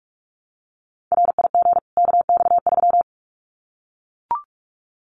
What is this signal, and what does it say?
Morse code, heard as a keyed tone, about 44 wpm, 700 Hz. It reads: RIG KX3
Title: Common QSO Phrases 45wpm